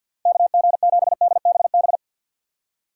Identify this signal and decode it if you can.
Morse code, heard as a keyed tone, about 61 wpm, 700 Hz. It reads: KG7DBB
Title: Speed Racing - USA 2x3 Call Signs 50wpm